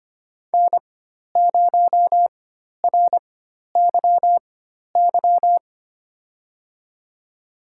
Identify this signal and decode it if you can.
Morse code, heard as a keyed tone, about 25 wpm, spaced as for 12 wpm, 700 Hz. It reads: N0RYY
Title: USA Call Signs - 5 Characters - 4x Character Spacing - 25wpm